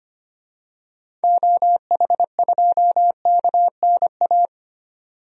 Morse code, heard as a keyed tone, about 25 wpm, 700 Hz. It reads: OH2KNA
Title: DXCC-100 Call Signs - 6 Characters - 25wpm